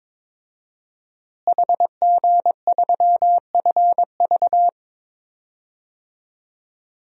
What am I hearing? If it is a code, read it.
Morse code, heard as a keyed tone, about 22 wpm, 700 Hz. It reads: HG3FV
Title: DXCC-100 Call Signs - 5 Characters - Repeated 3x - 22wpm